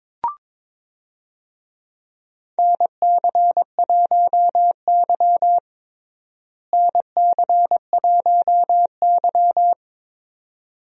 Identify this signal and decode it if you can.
Morse code, heard as a keyed tone, about 22 wpm, 700 Hz. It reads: NC1Y NC1Y
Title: USA Call Signs - 4 Characters - Repeated 2x - 22wpm